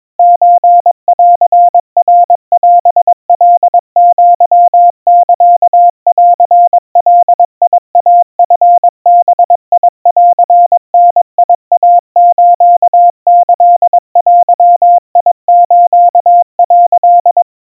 Morse code (Japanese Wabun code), heard as a keyed tone, about 22 wpm, 700 Hz. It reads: ソンナオカアサンガイチバンダイスキデス」
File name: 80 cpm 700 Hz Crispy